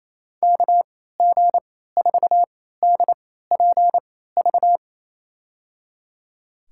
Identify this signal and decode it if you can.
Morse code, heard as a keyed tone, about 28 wpm, spaced as for 16 wpm, 700 Hz. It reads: KG4DPV
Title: USA Call Signs - 6 Characters - 3x Character Spacing - 28wpm